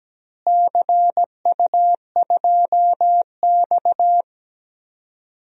Morse code, heard as a keyed tone, about 17 wpm, 700 Hz. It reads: CU2X
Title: DXCC-100 Call Signs - 4 Characters - Repeated 3x - 17wpm